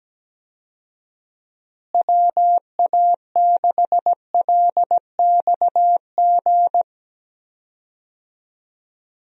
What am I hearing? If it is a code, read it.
Morse code, heard as a keyed tone, about 17 wpm, 700 Hz. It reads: WA6LXG
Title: USA Call Signs - 6 Characters - Repeated 3x - 17wpm